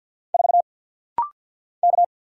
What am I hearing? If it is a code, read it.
Morse code, heard as a keyed tone, about 50 wpm, 690 Hz. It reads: 4 K
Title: Single Letter-Number Alt - Rapid-Fire 50wpm